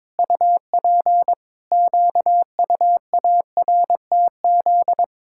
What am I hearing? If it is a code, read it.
Morse code, heard as a keyed tone, about 22 wpm, 700 Hz. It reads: UP QUARTZ
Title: Pangram Challenge v2 - 22wpm